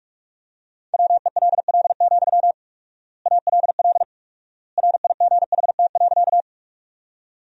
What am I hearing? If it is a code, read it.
Morse code, heard as a keyed tone, about 46 wpm, 700 Hz. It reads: WELL, ALL RIGHT.
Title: 4x Word-Spacing - Sentences from Top 200 Words - 45wpm